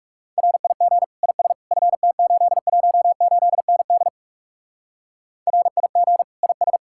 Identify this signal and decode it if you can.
Morse code, heard as a keyed tone, about 43 wpm, 700 Hz. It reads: RIG IS FT818ND RIG IS